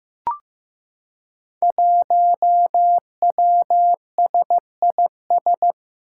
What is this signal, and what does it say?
Morse code, heard as a keyed tone, about 15 wpm, 700 Hz. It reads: 1WSIS
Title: Five Challenging Letter-Number Combinations 15wpm